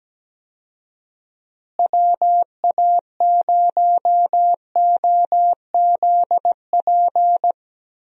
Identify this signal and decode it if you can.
Morse code, heard as a keyed tone, about 17 wpm, 700 Hz. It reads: WA0OZP
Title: USA Call Signs - 6 Characters - Repeated 2x - 17wpm